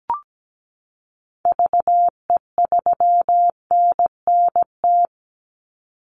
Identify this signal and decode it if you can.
Morse code, heard as a keyed tone, about 17 wpm, 700 Hz. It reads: VE3NNT
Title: Call Signs - 17wpm